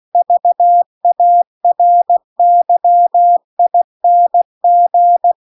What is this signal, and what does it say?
Morse code, heard as a keyed tone, about 16 wpm, 700 Hz. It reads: VARYING
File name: CW-The-Big-Twoheaded-River-1-16WPM